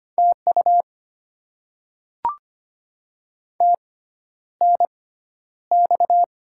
Morse code, heard as a keyed tone, about 25 wpm, spaced as for 11 wpm, 700 Hz. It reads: TU TNX
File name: Top 90 QSO Elements - 6x Character Spacing - 25wpm